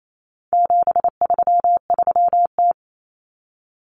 Morse code, heard as a keyed tone, about 28 wpm, 700 Hz. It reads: 733T
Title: Common Cut Numbers - 4-Digit 28wpm